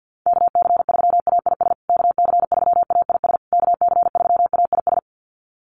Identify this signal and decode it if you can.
Morse code, heard as a keyed tone, about 49 wpm, 700 Hz. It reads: KC3AIS KC3AIS KC3AIS
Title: USA Call Signs Repeated 3x 50wpm